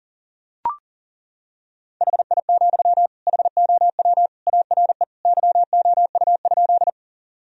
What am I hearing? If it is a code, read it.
Morse code, heard as a keyed tone, about 37 wpm, 700 Hz. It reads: HI, HOW ARE YOU?
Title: Sentences from Top 1000 Words - Review Part 3 - 40wpm